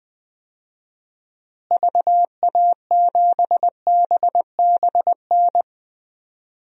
Morse code, heard as a keyed tone, about 20 wpm, 700 Hz. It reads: VA7BBN
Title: DXCC-100 Call Signs - 6 Characters - Repeated 3x - 20wpm